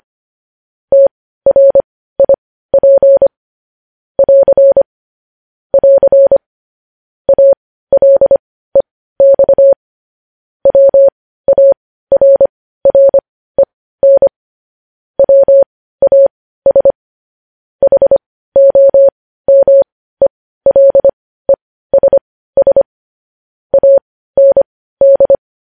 Morse code, heard as a keyed tone, about 25 wpm, spaced as for 15 wpm, 550 Hz. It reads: TRIP <AR> <AR> ALEX WARREN WAS HOMELESS AND
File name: podcast / News headlines-1x-25x15